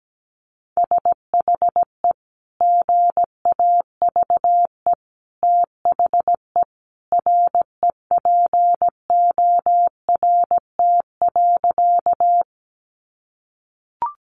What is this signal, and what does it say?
Morse code, heard as a keyed tone, about 17 wpm, 700 Hz. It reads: SHE GAVE THE REPORT.
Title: Sentences from Top 900 Words 17wpm